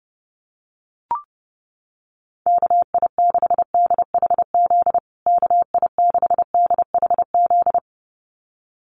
Morse code, heard as a keyed tone, about 30 wpm, 700 Hz. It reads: KI6DHZ KI6DHZ
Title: USA Call Signs Repeated 2x 30wpm